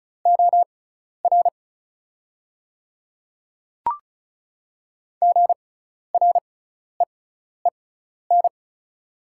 Morse code, heard as a keyed tone, about 34 wpm, spaced as for 12 wpm, 690 Hz. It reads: OR GREEN
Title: 6x Character Spacing - Top 500 Words - 35wpm